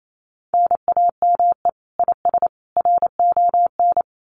Morse code, heard as a keyed tone, about 28 wpm, 700 Hz. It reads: NAME IS RON
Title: Common QSO Phrases 28wpm